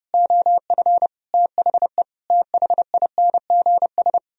Morse code, heard as a keyed tone, about 33 wpm, 700 Hz. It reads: OF THE THINGS